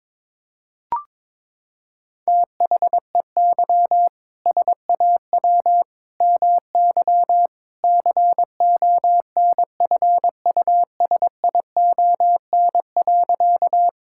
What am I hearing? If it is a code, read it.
Morse code, heard as a keyed tone, about 22 wpm, 700 Hz. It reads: THEY SAW MY CONFUSION.
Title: Sentences from Top 1600 Words 22wpm